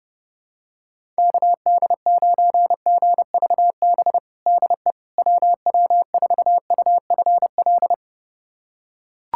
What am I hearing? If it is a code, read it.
Morse code, heard as a keyed tone, about 32 wpm, 700 Hz. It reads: KD9GVB DE WW4UFL
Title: USA Callsign DE Callsign - 6 Characters - 30wpm